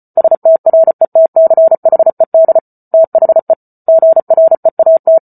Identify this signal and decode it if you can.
Morse code, heard as a keyed tone, about 34 wpm, 650 Hz. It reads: STRETCHED THE GREAT